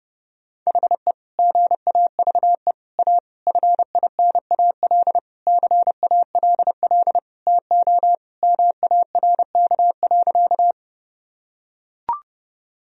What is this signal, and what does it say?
Morse code, heard as a keyed tone, about 30 wpm, 700 Hz. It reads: HE GAVE A FINAL CALL TO MARK.
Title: Sentences from Top 1000 Words - Review Part 2 - 30wpm